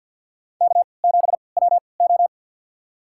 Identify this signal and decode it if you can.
Morse code, heard as a keyed tone, about 50 wpm, spaced as for 28 wpm, 700 Hz. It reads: K7WO
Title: USA Call Signs - 3x Character Spacing - 50wpm